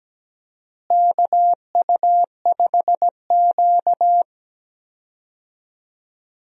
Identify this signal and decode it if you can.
Morse code, heard as a keyed tone, about 17 wpm, 700 Hz. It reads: KU5Q
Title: USA Call Signs - 4 Characters - Repeated 3x - 17wpm